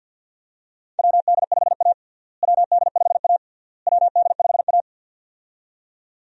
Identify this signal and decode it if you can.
Morse code, heard as a keyed tone, about 46 wpm, 700 Hz. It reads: WD5A WD5A WD5A
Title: USA Call Signs - 4 Characters - Repeated 3x - 50wpm